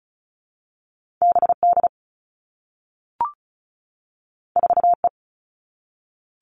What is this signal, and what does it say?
Morse code, heard as a keyed tone, about 36 wpm, 700 Hz. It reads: BD 4E
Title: All Cut Numbers - 2-Digit 35wpm